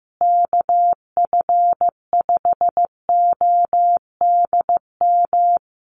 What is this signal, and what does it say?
Morse code, heard as a keyed tone, about 14 wpm, 700 Hz. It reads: KF5ODM
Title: USA Call Signs Repeated 3x 15wpm